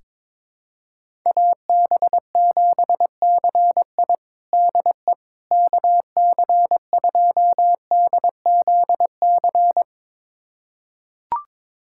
Morse code, heard as a keyed tone, about 22 wpm, 700 Hz. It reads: AB7CI DE KC2DZC
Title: USA Callsign DE Callsign 22wpm